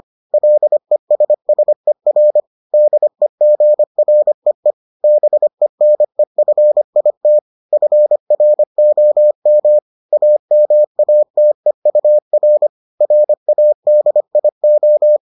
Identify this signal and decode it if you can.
Morse code, heard as a keyed tone, about 25 wpm, 600 Hz. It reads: LESSER DEGREE BENEFIT FROM AMATEUR RADIO